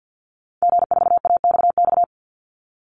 Morse code, heard as a keyed tone, about 50 wpm, 700 Hz. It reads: G4AXX
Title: Call Signs - 50wpm